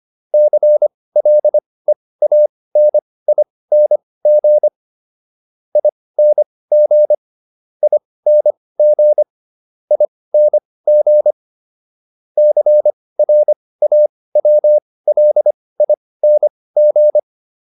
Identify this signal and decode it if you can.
Morse code, heard as a keyed tone, about 25 wpm, spaced as for 18 wpm, 600 Hz. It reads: CLEANING ING ING ING CRAWLING